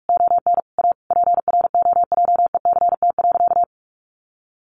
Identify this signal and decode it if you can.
Morse code, heard as a keyed tone, about 44 wpm, 700 Hz. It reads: ON A PROJECT.
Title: Sentences from Top 1400 Words 45wpm